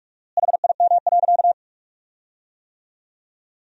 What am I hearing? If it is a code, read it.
Morse code, heard as a keyed tone, about 43 wpm, 700 Hz. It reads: HIM.
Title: 4x Word-Spacing - Sentences from Top 300 Words - 45wpm